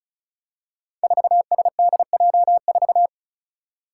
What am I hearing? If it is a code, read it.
Morse code, heard as a keyed tone, about 36 wpm, 700 Hz. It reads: 4SDJ4
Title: Five Challenging Letter-Number Combinations 35wpm